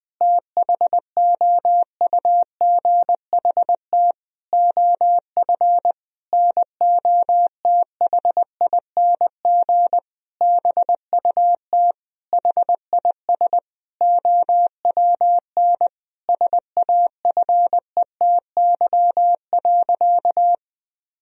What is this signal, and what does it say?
Morse code, heard as a keyed tone, about 19 wpm, 700 Hz. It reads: THOUGHT OF NOTHING BUT HIS OWN SAFETY.
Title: Sentences from Top 1000 Words - Review Part 4 - 20wpm